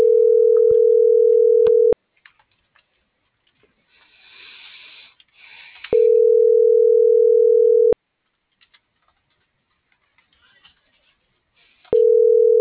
Ringback tone (North America), 440 + 480 Hz.